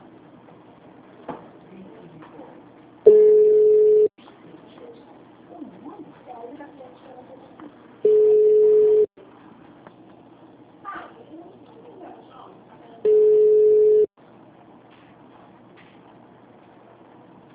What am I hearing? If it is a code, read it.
Ringback tone (Europe, 425 Hz), with speech.